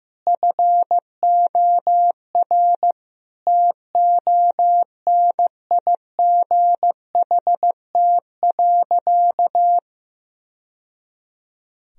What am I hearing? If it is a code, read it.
Morse code, heard as a keyed tone, about 16 wpm, 700 Hz. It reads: FOR TONIGHT.
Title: Sentences from Top 1000 Words - Review Part 3 - 15wpm